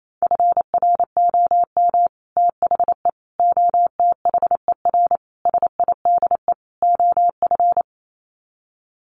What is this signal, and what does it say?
Morse code, heard as a keyed tone, about 28 wpm, 700 Hz. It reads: FROM THE OTHER SIDE OF
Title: Sets of 5 Words Encore 28wpm